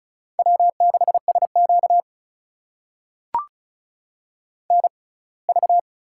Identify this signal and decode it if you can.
Morse code, heard as a keyed tone, about 36 wpm, spaced as for 24 wpm, 700 Hz. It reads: W6SQ NV
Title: USA Call Signs - 4 Characters - 6x Character Spacing - 35wpm